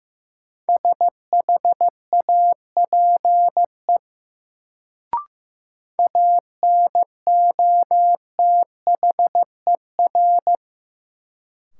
Morse code, heard as a keyed tone, about 15 wpm, 700 Hz. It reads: SHAPE ANOTHER
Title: Sets of 5 Words Encore - New Word Drill - 15wpm